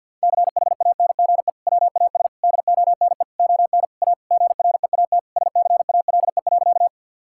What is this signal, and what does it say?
Morse code, heard as a keyed tone, about 50 wpm, 700 Hz. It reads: CHANGE WAS DONE ON A GREAT SCALE.
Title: Sentences from Top 1000 Words - Review Part 5 - 50wpm